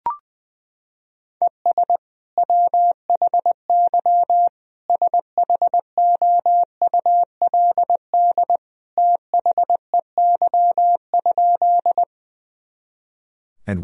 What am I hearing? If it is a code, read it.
Morse code, heard as a keyed tone, about 20 wpm, 700 Hz. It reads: ES WHY SHOULD THEY?
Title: Sentences from Top 300 Words and Abbreviations 20wpm